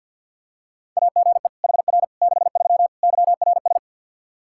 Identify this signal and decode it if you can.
Morse code, heard as a keyed tone, about 45 wpm, 700 Hz. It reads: AGE HR 63 YRS
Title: Practice QSO - Exchange 3 - More Info - 50wpm